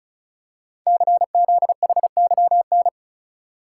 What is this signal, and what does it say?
Morse code, heard as a keyed tone, about 36 wpm, 700 Hz. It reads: CZHYN
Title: Code Groups - 5 Letters - 35wpm